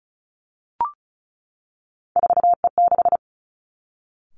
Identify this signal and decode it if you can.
Morse code, heard as a keyed tone, about 36 wpm, 700 Hz. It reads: 4E6
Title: Three Challenging Letter-Number Combinations 35wpm